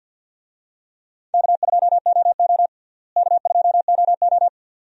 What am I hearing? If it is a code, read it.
Morse code, heard as a keyed tone, about 49 wpm, 700 Hz. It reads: K2OO K2OO
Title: USA Call Signs - 4 Characters - Repeated 2x - 50wpm